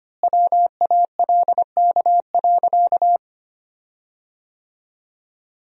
Morse code, heard as a keyed tone, about 25 wpm, 700 Hz. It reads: WALK.
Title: 3x Word-Spacing - Sentences from Top 300 Words - 25wpm